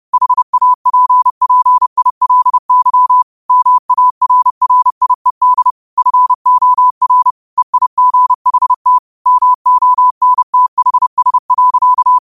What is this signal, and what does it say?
Morse code, heard as a keyed tone, about 30 wpm, 1000 Hz. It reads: HAPPILY MARRIED FOR EIGHT MONTHS.